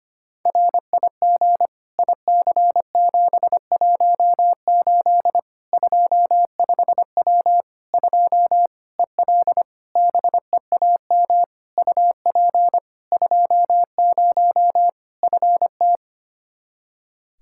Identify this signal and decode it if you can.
Morse code, heard as a keyed tone, about 25 wpm, 700 Hz. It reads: RIG IC718 25W 2 EL BEAM UP 20 FT
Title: Practice QSO - Exchange 2 - Info - 25wpm